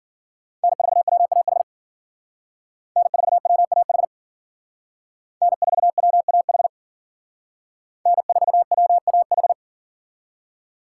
Morse code, heard as a keyed tone, about 52 wpm, 700 Hz. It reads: N4WAH N4WAH N4WAH N4WAH